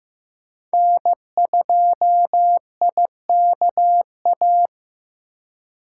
Morse code, heard as a keyed tone, about 15 wpm, 700 Hz. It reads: N2IKA
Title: USA Call Signs - 5 Characters - 2x Character Spacing - 15wpm